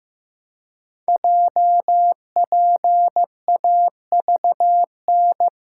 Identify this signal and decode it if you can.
Morse code, heard as a keyed tone, about 15 wpm, 700 Hz. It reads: JPAVN
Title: Code Groups - 5 Letters - 15wpm